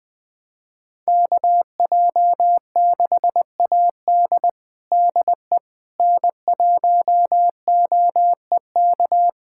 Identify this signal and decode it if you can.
Morse code, heard as a keyed tone, about 19 wpm, 700 Hz. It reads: KJ6AD DE N1OEK
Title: USA Callsign DE Callsign - 5 Characters - 20wpm